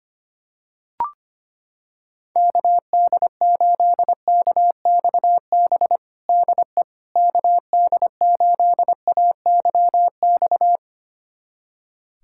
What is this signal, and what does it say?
Morse code, heard as a keyed tone, about 25 wpm, 700 Hz. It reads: KD8KXB DE KD8AYX
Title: USA Callsign DE Callsign 25wpm